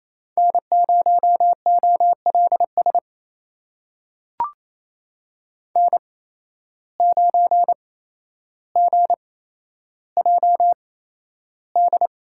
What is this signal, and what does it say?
Morse code, heard as a keyed tone, about 28 wpm, spaced as for 12 wpm, 700 Hz. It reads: N0OLS N9GJD
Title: USA Call Signs - 5 Characters - 8x Character Spacing - 28wpm